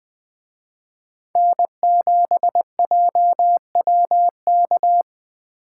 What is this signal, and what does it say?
Morse code, heard as a keyed tone, about 20 wpm, 700 Hz. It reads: N7JWK
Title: Speed Racing - USA 1x3 Call Signs 17wpm